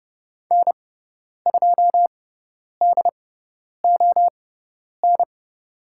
Morse code, heard as a keyed tone, about 30 wpm, spaced as for 10 wpm, 700 Hz. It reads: N2DON
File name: Lesson 16g - Letter c - Call Signs 30wpm